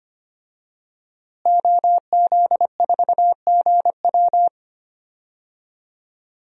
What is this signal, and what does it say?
Morse code, heard as a keyed tone, about 25 wpm, 700 Hz. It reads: OZ4GW
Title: DXCC-100 Call Signs - 5 Characters - Repeated 2x - 25wpm